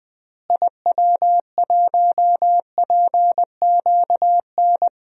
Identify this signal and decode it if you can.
Morse code, heard as a keyed tone, about 20 wpm, 700 Hz. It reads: IW1PQN
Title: DXCC-100 Call Signs - 6 Characters - Repeated 2x - 20wpm